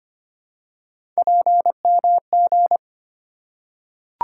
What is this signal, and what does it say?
Morse code, heard as a keyed tone, about 25 wpm, 700 Hz. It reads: PMG
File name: Three Letter-Number Combinations - 25wpm